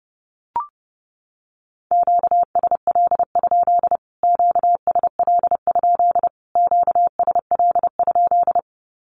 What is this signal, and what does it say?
Morse code, heard as a keyed tone, about 30 wpm, 700 Hz. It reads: QSL? QSL? QSL?
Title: Q-Codes Repeated 3x 30wpm